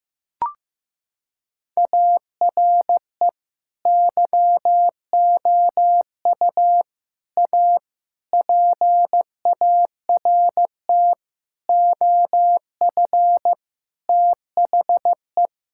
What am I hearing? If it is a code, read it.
Morse code, heard as a keyed tone, about 16 wpm, 700 Hz. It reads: ARE YOU A PART OF THE
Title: Sentences from Top 1600 Words 15wpm